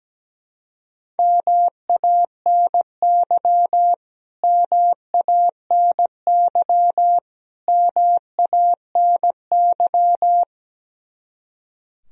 Morse code, heard as a keyed tone, about 17 wpm, 700 Hz. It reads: MANY MANY MANY
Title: Top 100 Words Repeated 17wpm